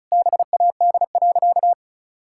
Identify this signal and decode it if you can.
Morse code, heard as a keyed tone, about 36 wpm, 700 Hz. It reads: BAD.
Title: Sentences from Top 1000 Words - Review Part 5 - 35wpm